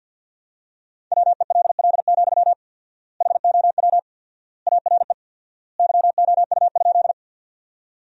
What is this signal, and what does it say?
Morse code, heard as a keyed tone, about 53 wpm, 700 Hz. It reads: WELL, HOW ARE YOU?
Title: 4x Word-Spacing - Sentences from Top 200 Words - 50wpm